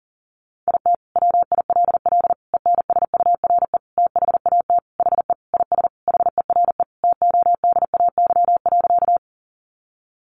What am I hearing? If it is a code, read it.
Morse code, heard as a keyed tone, about 37 wpm, 700 Hz. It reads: IT WILL ENSURE THAT HE IS HERE TODAY.
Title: Sentences from Top 1100 Words 40wpm